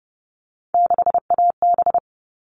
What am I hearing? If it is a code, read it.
Morse code, heard as a keyed tone, about 30 wpm, 700 Hz. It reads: 6AB
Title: All Cut Numbers - 3-Digit 30wpm